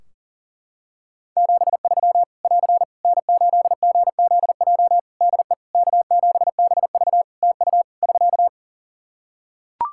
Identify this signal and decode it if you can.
Morse code, heard as a keyed tone, about 39 wpm, 700 Hz. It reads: 73 <AR> N8GZJ DE K7BV TU <SK>
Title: Practice QSO - Exchange 4 - Ending - 40wpm